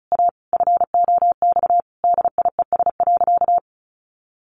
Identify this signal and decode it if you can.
Morse code, heard as a keyed tone, about 34 wpm, 700 Hz. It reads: A FOX DIES.
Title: A Sound Of Thunder 35wpm